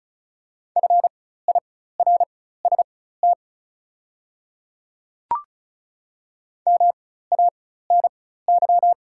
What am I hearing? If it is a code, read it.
Morse code, heard as a keyed tone, about 35 wpm, spaced as for 16 wpm, 700 Hz. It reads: FIRST MANY
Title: 4x Character Spacing - Top 200 Words - 35wpm